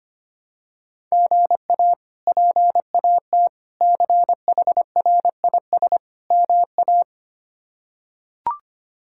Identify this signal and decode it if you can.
Morse code, heard as a keyed tone, about 25 wpm, 700 Hz. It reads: GA PAT CHRIS MA
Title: K1USN SST QSO Practice - 25wpm